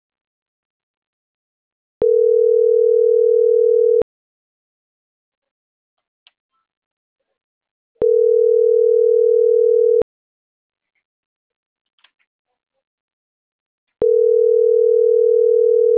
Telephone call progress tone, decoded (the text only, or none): ringback tone